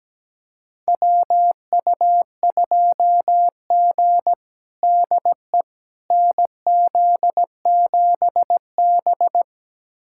Morse code, heard as a keyed tone, about 17 wpm, 700 Hz. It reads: WU2G DE NZ7B
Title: USA Callsign DE Callsign - 4 Characters - 17wpm